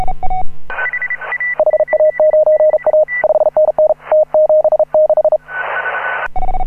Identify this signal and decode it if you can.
Morse code, heard as a keyed tone, about 35 wpm, 620 Hz. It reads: HA9A 5NN T76
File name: AUDIO2018 / HA9A CW